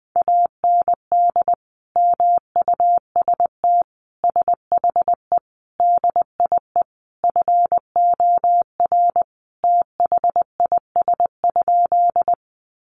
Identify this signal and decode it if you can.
Morse code, heard as a keyed tone, about 20 wpm, 700 Hz. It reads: AND MUST SHE DIE FOR THIS?
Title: Sentences from Top 1300 Words 20wpm